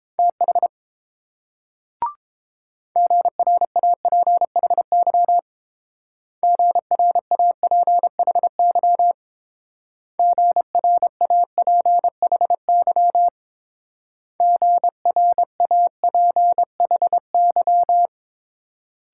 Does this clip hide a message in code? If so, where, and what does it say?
Yes: Morse code from 0.2 to 18.1 s, heard as a keyed tone, about 29 wpm, 700 Hz. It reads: TH GRAPHY GRAPHY GRAPHY GRAPHY